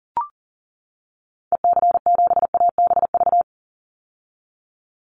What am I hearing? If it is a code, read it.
Morse code, heard as a keyed tone, about 40 wpm, 700 Hz. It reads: EC7ABV
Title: Call Signs - 40wpm